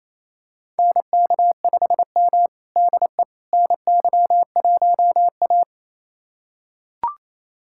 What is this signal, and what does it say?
Morse code, heard as a keyed tone, about 28 wpm, 700 Hz. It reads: NK5M DE NY1A
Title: USA Callsign DE Callsign - 4 Characters - 28wpm